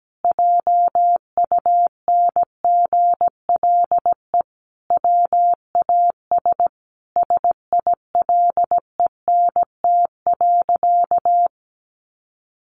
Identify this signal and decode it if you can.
Morse code, heard as a keyed tone, about 17 wpm, 700 Hz. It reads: JUNGLE WAS SILENT.
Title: A Sound Of Thunder 17wpm